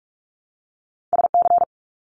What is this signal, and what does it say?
Morse code, heard as a keyed tone, about 44 wpm, 700 Hz. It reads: SC
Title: US States Repeated 3x 45wpm